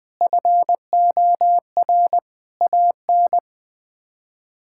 Morse code, heard as a keyed tone, about 20 wpm, 700 Hz. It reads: FOR AN
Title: Sets of 6 Words - 20wpm